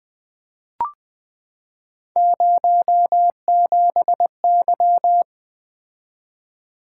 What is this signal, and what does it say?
Morse code, heard as a keyed tone, about 20 wpm, 700 Hz. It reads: 07Y
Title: Three Letter-Number Combinations - 20wpm